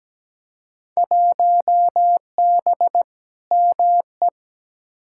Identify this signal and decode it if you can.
Morse code, heard as a keyed tone, about 17 wpm, 700 Hz. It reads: 1B ME
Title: Field Day Exchanges - 17wpm